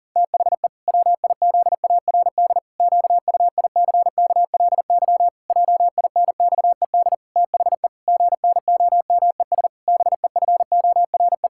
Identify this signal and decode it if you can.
Morse code, heard as a keyed tone, about 37 wpm, 700 Hz. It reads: THE WIZARD QUICKLY JINXED THE GNOMES BEFORE
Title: Pangram Challenge v2 - 40wpm